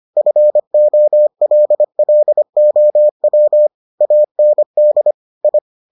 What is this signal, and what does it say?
Morse code, heard as a keyed tone, about 25 wpm, 600 Hz. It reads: FOLLOW AND I